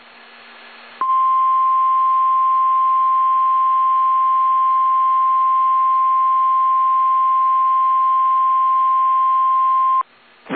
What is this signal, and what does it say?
Emergency Alert System (SAME) tones (the 1050 Hz attention tone). It is heard from 1.0 to 10.1 s.